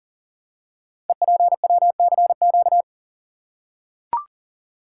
Morse code, heard as a keyed tone, about 39 wpm, 690 Hz. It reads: EPWCQ
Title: Code Groups - 5 Letter:Numbers - 40wpm